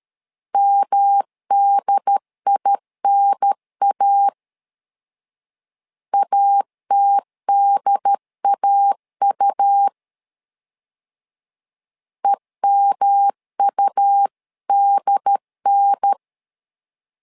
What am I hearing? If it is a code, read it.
Morse code, heard as a keyed tone, about 13 wpm, 800 Hz. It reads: MDINA ATDAU EMUDN
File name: LEZIONI / 3 MWRO / 3 M 12wpm 1541